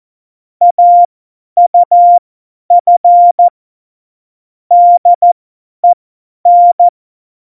Morse code, heard as a keyed tone, about 14 wpm, spaced as for 10 wpm, 700 Hz. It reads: AUF DEN
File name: Morsekurs / Witze / Witz 24. November 2020, 50 F70